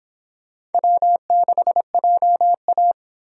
Morse code, heard as a keyed tone, about 29 wpm, 700 Hz. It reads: W6JA